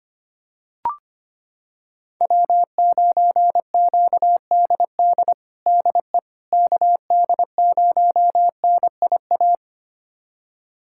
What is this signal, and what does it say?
Morse code, heard as a keyed tone, about 25 wpm, 700 Hz. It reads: W9QDD DE KD0NIA
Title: USA Callsign DE Callsign 25wpm